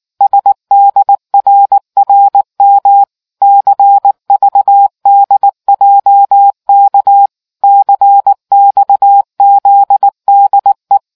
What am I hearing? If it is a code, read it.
Morse code, heard as a keyed tone, about 19 wpm, 800 Hz. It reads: SDRRM CVDJK CXZDE